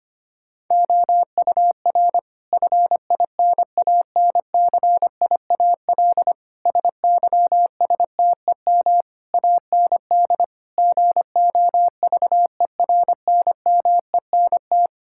Morse code, heard as a keyed tone, about 25 wpm, 700 Hz. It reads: OUR FINANCIAL SYSTEM AND GOVERNMENT
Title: Sentences from Top 1000 Words - Review Part 5 - 25wpm